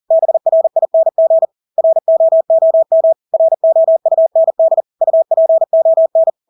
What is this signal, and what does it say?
Morse code, heard as a keyed tone, about 40 wpm, 650 Hz. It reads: BRING ROOM ROUND UPON
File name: Engelske ord 40-40 wpm